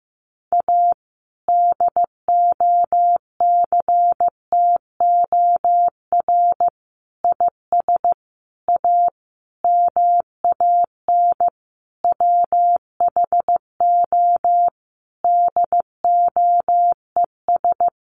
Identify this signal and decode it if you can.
Morse code, heard as a keyed tone, about 15 wpm, 700 Hz. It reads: A DOCTOR IS A MAN WHO DOES
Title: Sentences from Top 1500 Words 15wpm